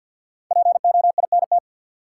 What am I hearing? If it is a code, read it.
Morse code, heard as a keyed tone, about 53 wpm, 700 Hz. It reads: POINT
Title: Sets of 3 Words - New Word Drill - 50wpm